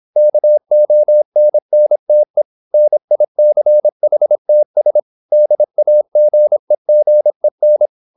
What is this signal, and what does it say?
Morse code, heard as a keyed tone, about 26 wpm, 600 Hz. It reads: KONNTE NICHTS DAGEGEN